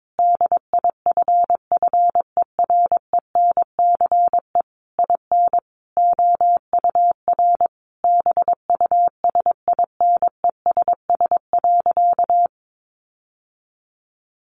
Morse code, heard as a keyed tone, about 22 wpm, 700 Hz. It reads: DIFFERENCE IN OUR BUSINESS.
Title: Sentences from Top 1400 Words 22wpm